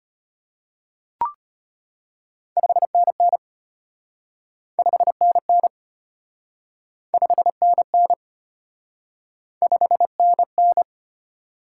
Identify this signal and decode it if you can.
Morse code, heard as a keyed tone, about 32 wpm, 700 Hz. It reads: 5NN 5NN 5NN 5NN